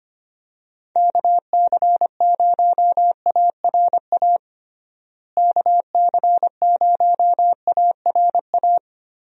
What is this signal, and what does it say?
Morse code, heard as a keyed tone, about 25 wpm, 700 Hz. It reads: KC0ARA KC0ARA
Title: USA Call Signs - 6 Characters - Repeated 2x - 25wpm